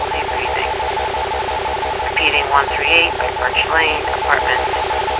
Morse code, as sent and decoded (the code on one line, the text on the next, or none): none